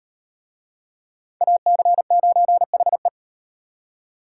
Morse code, heard as a keyed tone, about 37 wpm, 700 Hz. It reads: AC9HE